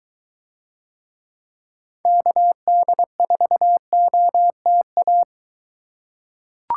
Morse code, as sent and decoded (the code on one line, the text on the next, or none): -.- -.. ....- --- - .-
KD4OTA